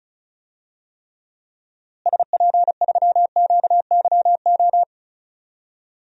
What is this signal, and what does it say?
Morse code, heard as a keyed tone, about 35 wpm, 700 Hz. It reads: SP3QYO
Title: DXCC-100 Call Signs - 6 Characters - Repeated 3x - 35wpm